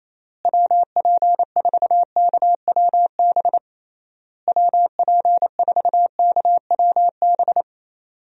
Morse code, heard as a keyed tone, about 28 wpm, 700 Hz. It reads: WP4KWB WP4KWB
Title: USA Call Signs - 6 Characters - Repeated 3x - 28wpm